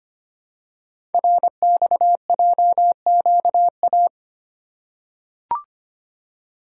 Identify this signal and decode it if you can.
Morse code, heard as a keyed tone, about 25 wpm, 700 Hz. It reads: RXJQA
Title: Code Groups - 5 Letters - 25wpm